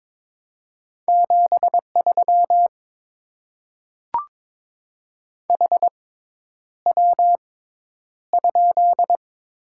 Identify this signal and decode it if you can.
Morse code, heard as a keyed tone, about 22 wpm, spaced as for 10 wpm, 700 Hz. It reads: 73 HW?
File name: Top 20 QSO Elements - 6x Character Spacing - 22wpm